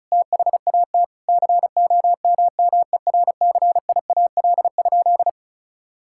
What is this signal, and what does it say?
Morse code, heard as a keyed tone, about 36 wpm, 700 Hz. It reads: THAT COMMERCIAL?
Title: Sentences from Top 1000 Words - Review Part 5 - 35wpm